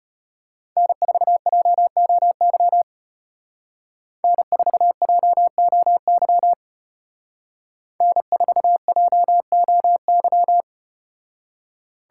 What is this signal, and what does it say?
Morse code, heard as a keyed tone, about 34 wpm, 700 Hz. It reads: N4JOY N4JOY N4JOY